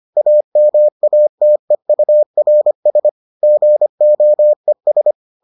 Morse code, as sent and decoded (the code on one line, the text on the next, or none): .- -- .- - . ..- .-. ... / --. --- . ...
AMATEURS GOES